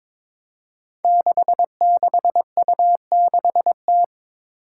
Morse code, heard as a keyed tone, about 22 wpm, 700 Hz. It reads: 66U6T
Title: Five Challenging Letter-Number Combinations 22wpm